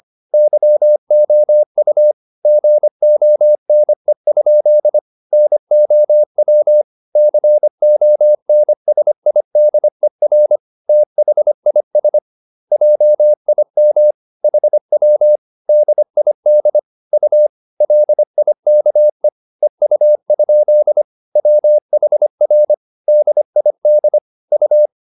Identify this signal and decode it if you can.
Morse code, heard as a keyed tone, about 25 wpm, 600 Hz. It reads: YOU GONE? NOW CONSIDER THIS JIM HW DID U LIKE EU? WHR DID U